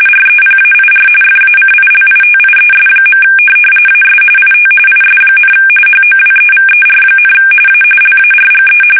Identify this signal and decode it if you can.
RTTY, about 45.45 baud. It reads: REPAIR FACILITIES. AS IT IS, HOWEVER, A LARGE
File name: fsk850